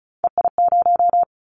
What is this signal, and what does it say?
Morse code, heard as a keyed tone, about 36 wpm, 700 Hz. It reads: EI0
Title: Callsign Prefixes for Top 50 DXCC - 35wpm